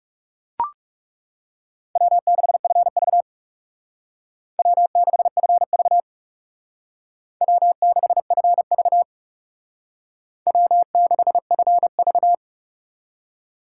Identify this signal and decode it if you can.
Morse code, heard as a keyed tone, about 39 wpm, 700 Hz. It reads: W6FV W6FV W6FV W6FV